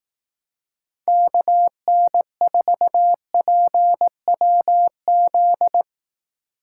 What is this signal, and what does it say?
Morse code, heard as a keyed tone, about 18 wpm, 700 Hz. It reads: KN4PWZ
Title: Speed Racing - USA Call Signs 15wpm